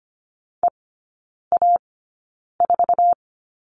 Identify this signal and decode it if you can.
Morse code, heard as a keyed tone, about 25 wpm, spaced as for 9 wpm, 690 Hz. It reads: EA4
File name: Lesson 14e - Number 2 - Three Characters 25wpm